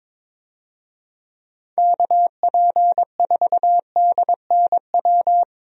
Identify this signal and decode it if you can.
Morse code, heard as a keyed tone, about 22 wpm, 700 Hz. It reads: KP4DNW
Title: USA Call Signs - 6 Characters - Repeated 3x - 22wpm